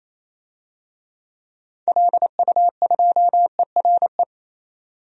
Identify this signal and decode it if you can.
Morse code, heard as a keyed tone, about 28 wpm, 700 Hz. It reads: LU2ERE
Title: DXCC-100 Call Signs - 6 Characters - 28wpm